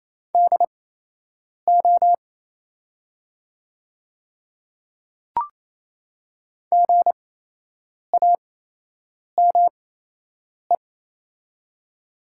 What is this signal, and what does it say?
Morse code, heard as a keyed tone, about 28 wpm, spaced as for 8 wpm, 700 Hz. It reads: DO GAME
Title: 8x Character Spacing - Top 500 Words - 28wpm